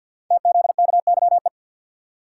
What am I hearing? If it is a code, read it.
Morse code, heard as a keyed tone, about 53 wpm, 700 Hz. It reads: TZKYE
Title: Code Groups - 5 Letters - 50wpm